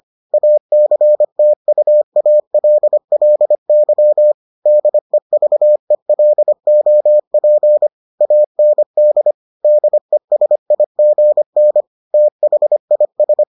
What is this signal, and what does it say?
Morse code, heard as a keyed tone, about 25 wpm, 600 Hz. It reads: ACTUALLY DEVELOP AND DESIGN THIS